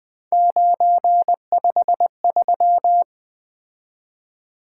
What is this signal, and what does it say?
Morse code, heard as a keyed tone, about 20 wpm, 700 Hz. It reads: 953
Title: Three Letter-Number Combinations - 20wpm